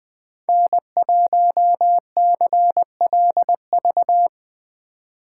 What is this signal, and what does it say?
Morse code, heard as a keyed tone, about 20 wpm, 700 Hz. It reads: N1CLV